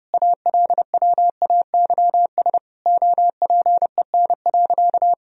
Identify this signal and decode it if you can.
Morse code, heard as a keyed tone, about 30 wpm, 700 Hz. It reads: ALWAYS OPEN.
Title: Sentences from Top 1000 Words - Review Part 1 - 30wpm